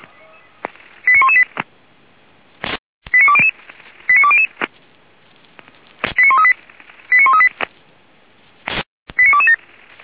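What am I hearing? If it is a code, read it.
Five-tone selective call (ZVEI), heard as quick runs of five tones, tones of about 80 ms. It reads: 89118 89200 89200 89138 89138 89227
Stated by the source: Five-tone ZVEI selective call 85724, received by SDR.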